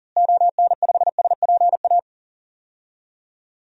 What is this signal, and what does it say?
Morse code, heard as a keyed tone, about 40 wpm, 700 Hz. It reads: ON5SPA
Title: DXCC-100 Call Signs - 6 Characters - Repeated 2x - 40wpm